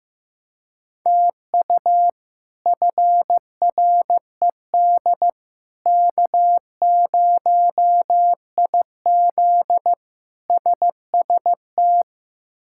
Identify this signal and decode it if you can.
Morse code, heard as a keyed tone, about 16 wpm, 700 Hz. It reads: TU FRED K0IZ SST
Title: K1USN SST QSO Practice - 15wpm